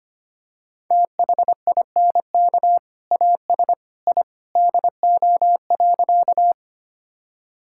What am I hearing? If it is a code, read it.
Morse code, heard as a keyed tone, about 25 wpm, 700 Hz. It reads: THINK AS I DO.
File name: practice / 25wpm / Sentences with 2-Word Phrases - 25wpm